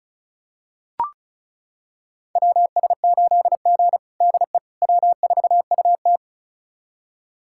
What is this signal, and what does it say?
Morse code, heard as a keyed tone, about 36 wpm, 700 Hz. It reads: WS8G DE W4UT
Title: USA Callsign DE Callsign - 4 Characters - 35wpm